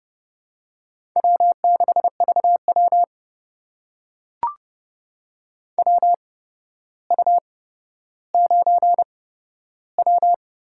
Morse code, heard as a keyed tone, about 30 wpm, spaced as for 13 wpm, 700 Hz. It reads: W6VW WU9W
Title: USA Call Signs - 4 Characters - 8x Character Spacing - 30wpm